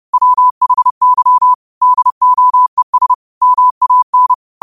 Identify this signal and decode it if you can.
Morse code, heard as a keyed tone, about 30 wpm, 1000 Hz. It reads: WHY DOES MAN
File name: qotdfiles / 30wpm